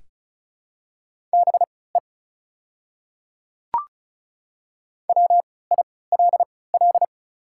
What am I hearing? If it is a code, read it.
Morse code, heard as a keyed tone, about 36 wpm, spaced as for 20 wpm, 690 Hz. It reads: BE WILL
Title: 3x Character Spacing - Top 100 Words - 35wpm